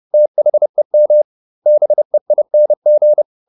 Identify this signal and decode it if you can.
Morse code, heard as a keyed tone, about 30 wpm, 600 Hz. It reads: THEM BEING